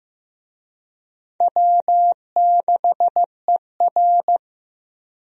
Morse code, heard as a keyed tone, about 16 wpm, 700 Hz. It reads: W6ER
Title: USA Call Signs - 4 Characters - Repeated 2x - 15wpm